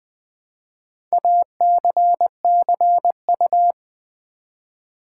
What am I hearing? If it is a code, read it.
Morse code, heard as a keyed tone, about 20 wpm, 700 Hz. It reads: ACCU
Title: Build A Word - 2 Letters At A Time - 6-Letter Words - 20wpm